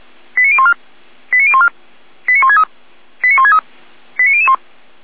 Five-tone selective call (ZVEI), heard as quick runs of five tones, tones of about 70 ms. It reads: 89014 89014 89163 89163 89001